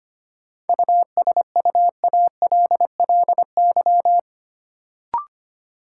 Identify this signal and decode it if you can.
Morse code, heard as a keyed tone, about 25 wpm, 700 Hz. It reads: USUALLY
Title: Sentences from Top 1000 Words - Review Part 3 - New Word Drill - 25wpm